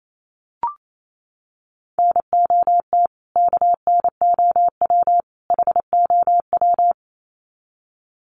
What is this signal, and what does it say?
Morse code, heard as a keyed tone, about 28 wpm, 700 Hz. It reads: NOT KNOW HOW
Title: Sets of 3 Words 28wpm